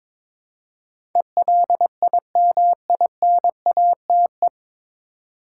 Morse code, heard as a keyed tone, about 22 wpm, 700 Hz. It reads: ELIMINATE
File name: 9 Letter Words - First 3 Letters Given - 22wpm